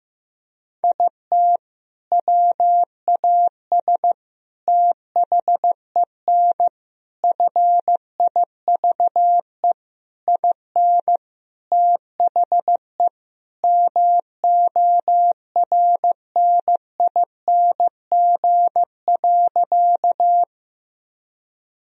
Morse code, heard as a keyed tone, about 15 wpm, 700 Hz. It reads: IT WAS THEN FIVE IN THE MORNING.
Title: Sentences from Top 1000 Words - Review Part 2 - 15wpm